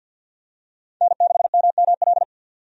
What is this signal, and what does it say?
Morse code, heard as a keyed tone, about 51 wpm, 700 Hz. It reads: N6MML
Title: USA Call Signs - 5 Characters - 50wpm